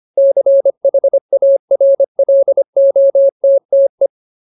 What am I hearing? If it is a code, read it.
Morse code, heard as a keyed tone, about 25 wpm, 550 Hz. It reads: CHARLOTTE
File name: Ultimate Word List / 25WPM Ultimate